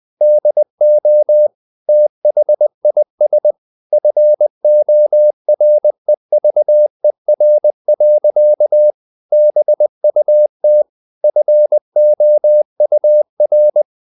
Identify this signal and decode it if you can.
Morse code, heard as a keyed tone, about 20 wpm, 600 Hz. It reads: DO THIS FOREVER. BUT FOUR
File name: the old man and the sea / the old man and the sea part 1 20wpm